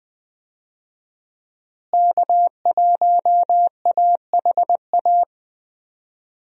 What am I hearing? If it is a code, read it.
Morse code, heard as a keyed tone, about 21 wpm, 700 Hz. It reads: K1AHA
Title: USA Call Signs - 5 Characters - Repeated 3x - 20wpm